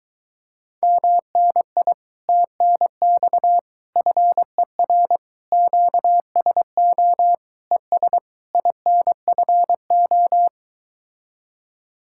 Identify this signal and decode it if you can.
Morse code, heard as a keyed tone, about 24 wpm, 700 Hz. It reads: MNI TNX FER QSO ES INFO